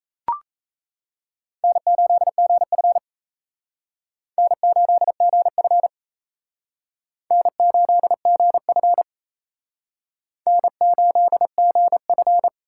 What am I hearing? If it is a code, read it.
Morse code, heard as a keyed tone, about 37 wpm, 700 Hz. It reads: N8GF N8GF N8GF N8GF